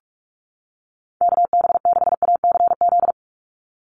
Morse code, heard as a keyed tone, about 46 wpm, 690 Hz. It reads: KB6ACZ
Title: USA Call Signs 45wpm